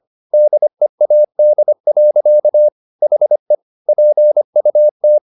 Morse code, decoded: DEAD. HE PUT

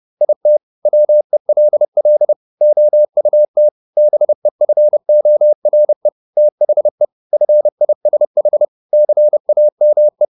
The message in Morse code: IT WELL OUT BEFORE THE FISH CAME

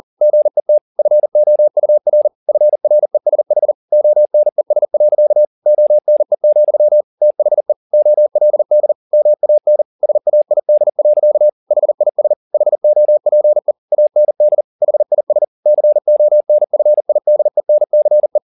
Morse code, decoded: GET FOUR FRESH ONES. ONE, THE OLD MAN SAID. HIS HOPE AND HIS CONFIDENCE

